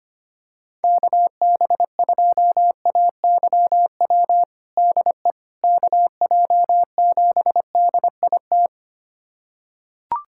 Morse code, heard as a keyed tone, about 25 wpm, 700 Hz. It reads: KB2AYW DE KJ7DIT
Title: USA Callsign DE Callsign - 6 Characters - 25wpm